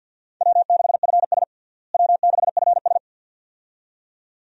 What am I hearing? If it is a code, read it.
Morse code, heard as a keyed tone, about 53 wpm, 700 Hz. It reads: W6FS W6FS
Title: USA Call Signs - 4 Characters - Repeated 2x - 50wpm